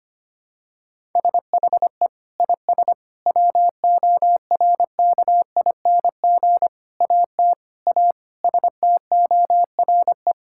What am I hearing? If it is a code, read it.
Morse code, heard as a keyed tone, about 25 wpm, 700 Hz. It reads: SHE IS WORKING AT A STORE
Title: Sentences from Top 1000 Words - Review Part 5 - 25wpm